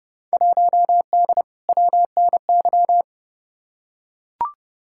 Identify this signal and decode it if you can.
Morse code, heard as a keyed tone, about 30 wpm, 700 Hz. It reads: 1D WNY
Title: Field Day Exchanges - 30wpm